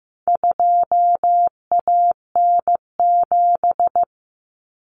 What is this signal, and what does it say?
Morse code, heard as a keyed tone, about 15 wpm, 700 Hz. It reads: 2AN7
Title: Common Cut Numbers - 4-Digit 15wpm